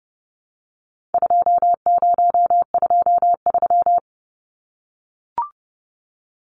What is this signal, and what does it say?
Morse code, heard as a keyed tone, about 30 wpm, 700 Hz. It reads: 2023
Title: Numbers - 4-Digit 30wpm